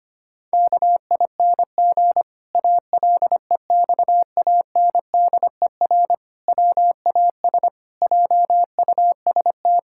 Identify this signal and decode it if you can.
Morse code, heard as a keyed tone, about 25 wpm, 700 Hz. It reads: KING ALEXANDER WAS JUST
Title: Pangram Challenge v2 - 25wpm